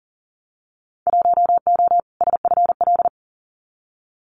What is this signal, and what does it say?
Morse code, heard as a keyed tone, about 40 wpm, 700 Hz. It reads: 1O SFL
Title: Winter Field Day Exchanges 40wpm